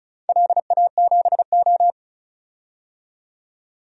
Morse code, heard as a keyed tone, about 36 wpm, 700 Hz. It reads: LA7O
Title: DXCC-100 Call Signs - 4 Characters - 35wpm